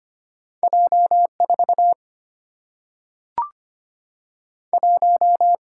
Morse code, heard as a keyed tone, about 25 wpm, 700 Hz. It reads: J 4 1
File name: Two Letter-Number Combinations - 25wpm